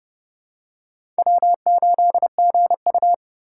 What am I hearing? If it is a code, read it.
Morse code, heard as a keyed tone, about 30 wpm, 700 Hz. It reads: W8GU
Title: K1USN SST QSO Practice - 30wpm